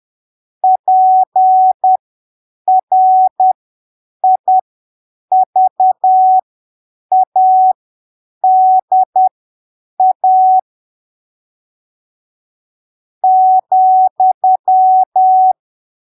Morse code, heard as a keyed tone, about 10 wpm, spaced as for 7 wpm, 750 Hz. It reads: PRIVADA ,
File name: arquivos / texto / texto 28 10ppm